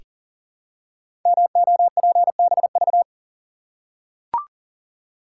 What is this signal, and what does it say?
Morse code, heard as a keyed tone, about 41 wpm, 700 Hz. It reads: MOPBV
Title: Code Groups - 5 Letters - 40wpm